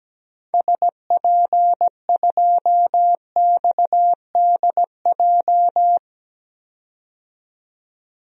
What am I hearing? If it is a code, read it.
Morse code, heard as a keyed tone, about 17 wpm, 700 Hz. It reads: SP2XDJ
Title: DXCC-100 Call Signs - 6 Characters - 17wpm